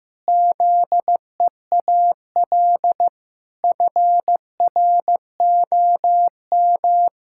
Morse code, heard as a keyed tone, about 15 wpm, 700 Hz. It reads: ZEAL FROM
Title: Pangram Challenge v2 - 15wpm